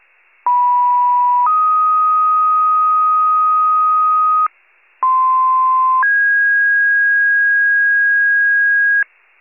Two-tone page, heard as a one-second tone, then a three-second tone, about 980 then 1250 Hz, about 1040 then 1650 Hz, with noise. It is heard from 0.5 to 9.0 s.